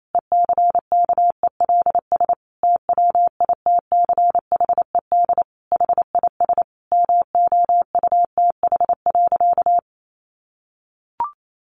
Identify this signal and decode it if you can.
Morse code, heard as a keyed tone, about 28 wpm, 700 Hz. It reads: ECKELS TWITCHED HIS MOUTH.
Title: A Sound Of Thunder 28wpm